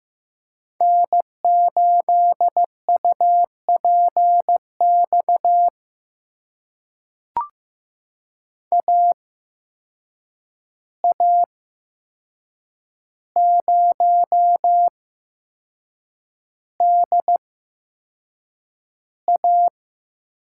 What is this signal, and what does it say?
Morse code, heard as a keyed tone, about 15 wpm, spaced as for 6 wpm, 700 Hz. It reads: N8UPX AA0DA
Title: USA Call Signs - 5 Characters - 8x Character Spacing - 15wpm